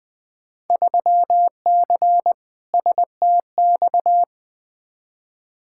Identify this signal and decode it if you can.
Morse code, heard as a keyed tone, about 20 wpm, 700 Hz. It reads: 3C STX
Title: Field Day Exchanges Encore - 20wpm